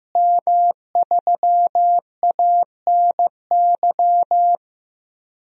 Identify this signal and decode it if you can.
Morse code, heard as a keyed tone, about 15 wpm, 700 Hz. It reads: M3ANY
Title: DXCC-100 Call Signs - 5 Characters - Repeated 3x - 15wpm